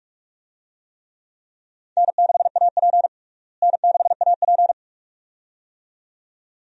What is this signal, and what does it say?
Morse code, heard as a keyed tone, about 42 wpm, 700 Hz. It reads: N6AP N6AP
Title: USA Call Signs - 4 Characters - Repeated 2x - 45wpm